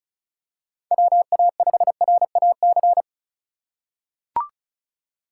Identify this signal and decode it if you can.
Morse code, heard as a keyed tone, about 36 wpm, 700 Hz. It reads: WA5RAC
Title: USA Call Signs - 6 Characters - 2x Character Spacing - 35wpm